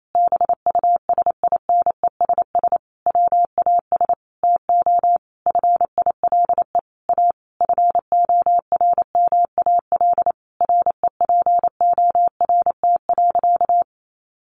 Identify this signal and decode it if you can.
Morse code, heard as a keyed tone, about 28 wpm, 700 Hz. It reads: BUSINESS WAS TO FILE A FORMAL REPORT.
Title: Sentences from Top 1300 Words 28wpm